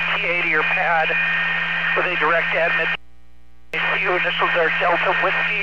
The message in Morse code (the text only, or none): none